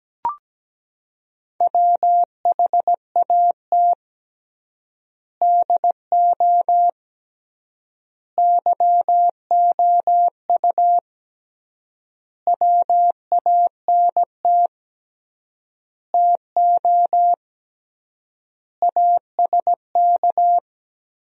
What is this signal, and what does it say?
Morse code, heard as a keyed tone, about 17 wpm, 700 Hz. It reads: WHAT DO YOU WANT TO ASK
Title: 3x Word-Spacing - Sentences from Top 200 Words - 17wpm